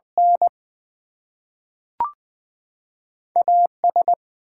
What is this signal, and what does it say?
Morse code, heard as a keyed tone, about 20 wpm, 700 Hz. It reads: N AS
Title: Two Letter Words - 20wpm